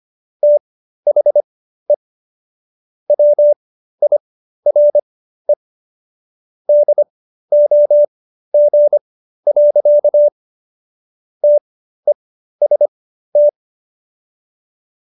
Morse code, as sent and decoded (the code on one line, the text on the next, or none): - .... / .-- .. .-. / -.. --- --. .-.-.- / - . ... -
THE WIRE DOG. TEST